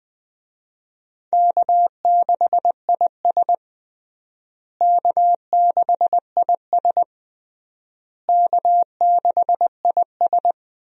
Morse code, heard as a keyed tone, about 20 wpm, 700 Hz. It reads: K6IS K6IS K6IS
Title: USA Call Signs - 4 Characters - Repeated 3x - 20wpm